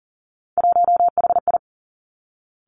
Morse code, heard as a keyed tone, about 41 wpm, 700 Hz. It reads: 1HI
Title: Three Challenging Letter-Number Combinations 40wpm